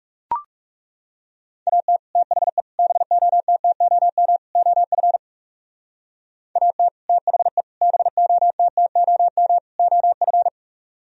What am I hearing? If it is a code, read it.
Morse code, heard as a keyed tone, about 36 wpm, 700 Hz. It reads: AT THE BOTTOM OF AT THE BOTTOM OF
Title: Speed-Racing - Sets of 4 Words 30wpm